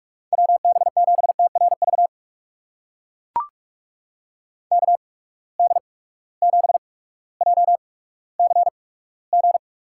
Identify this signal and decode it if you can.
Morse code, heard as a keyed tone, about 44 wpm, spaced as for 19 wpm, 700 Hz. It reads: WB7TRV KD7JCG
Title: USA Call Signs - 6 Characters - 8x Character Spacing - 45wpm